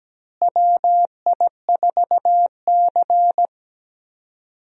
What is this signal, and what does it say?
Morse code, heard as a keyed tone, about 17 wpm, 700 Hz. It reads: WI4C
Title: USA Call Signs - 4 Characters - Repeated 3x - 17wpm